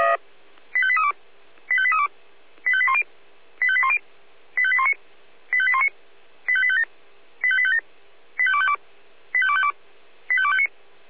Five-tone selective call (ZVEI), heard as quick runs of five tones, tones of about 70 ms. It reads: 85832 85832 85810 85810 85819 85819 85857 85857 85252 85252 85259